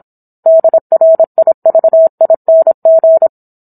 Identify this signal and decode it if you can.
Morse code, heard as a keyed tone, about 27 wpm, 650 Hz. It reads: DRIVING
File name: Book One / 17-Chapter 17 - 26 WPM